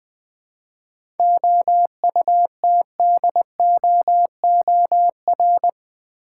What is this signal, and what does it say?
Morse code, heard as a keyed tone, about 20 wpm, 700 Hz. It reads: OUTDOOR
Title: 7 Letter Words - First 3 Letters Given - 20wpm